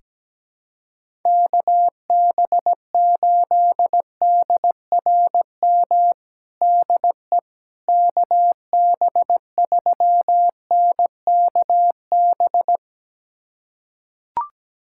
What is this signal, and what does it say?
Morse code, heard as a keyed tone, about 17 wpm, 700 Hz. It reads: KB8DRM DE KB3NKB
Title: USA Callsign DE Callsign - 6 Characters - 17wpm